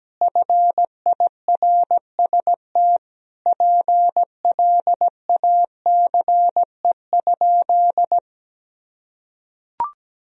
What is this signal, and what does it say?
Morse code, heard as a keyed tone, about 17 wpm, 700 Hz. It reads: FIRST PLACE?
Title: Sentences from Top 200 Words and Abbreviations 17wpm